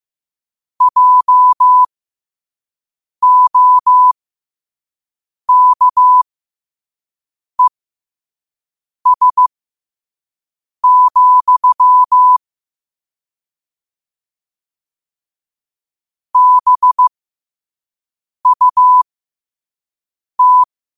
Morse code, heard as a keyed tone, about 15 wpm, spaced as for 5 wpm, 1000 Hz. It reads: JOKES, BUT
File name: 5wpm